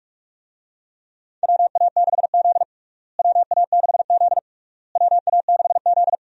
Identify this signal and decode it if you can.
Morse code, heard as a keyed tone, about 46 wpm, 700 Hz. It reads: WA6Z WA6Z WA6Z
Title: USA Call Signs - 4 Characters - Repeated 3x - 45wpm